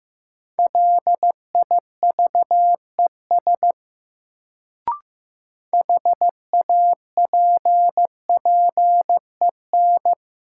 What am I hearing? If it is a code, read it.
Morse code, heard as a keyed tone, about 15 wpm, 700 Hz. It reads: LIVES HAPPEN
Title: Sets of 5 Words Encore - New Word Drill - 15wpm